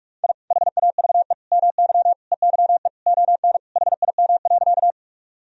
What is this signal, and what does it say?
Morse code, heard as a keyed tone, about 46 wpm, 700 Hz. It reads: I HAVE MY EYE ON HIM.
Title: Sentences from Top 1000 Words - Review Part 1 - 45wpm